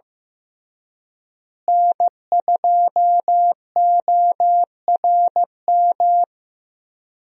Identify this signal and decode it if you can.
Morse code, heard as a keyed tone, about 15 wpm, 700 Hz. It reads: N2ORM
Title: USA Call Signs - 5 Characters - Repeated 3x - 15wpm